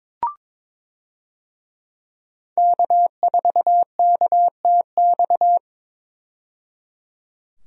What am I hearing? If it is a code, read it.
Morse code, heard as a keyed tone, about 21 wpm, 700 Hz. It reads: K4KTX
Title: USA Call Signs - 5 Characters - 22wpm